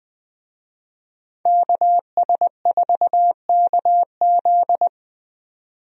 Morse code, heard as a keyed tone, about 20 wpm, 700 Hz. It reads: KS4KZ
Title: USA Call Signs - 5 Characters - 20wpm